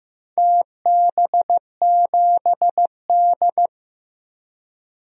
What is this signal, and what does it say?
Morse code, heard as a keyed tone, about 15 wpm, 700 Hz. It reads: TB7D
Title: DXCC-100 Call Signs - 4 Characters - 15wpm